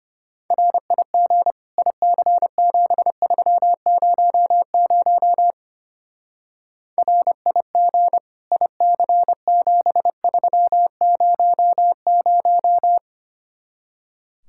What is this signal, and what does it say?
Morse code, heard as a keyed tone, about 29 wpm, 700 Hz. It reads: RIG IC7300 RIG IC7300
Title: Speed-Racing Common QSO Phrases Encore - 25wpm